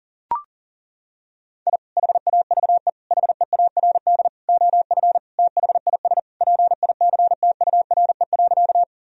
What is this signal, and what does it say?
Morse code, heard as a keyed tone, about 37 wpm, 700 Hz. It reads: I HAVE HEARD OF THIS PICTURE.
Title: Sentences from Top 1000 Words - Review Part 1 - 40wpm